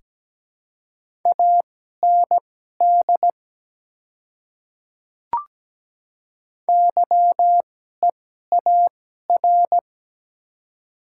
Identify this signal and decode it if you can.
Morse code, heard as a keyed tone, about 17 wpm, spaced as for 12 wpm, 700 Hz. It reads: AND YEAR
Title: 2x Character Spacing - Top 200 Words - 17wpm